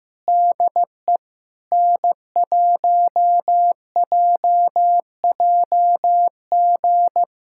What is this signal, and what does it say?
Morse code, heard as a keyed tone, about 15 wpm, 700 Hz. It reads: DE N1JJG
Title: USA Callsign DE Callsign 15wpm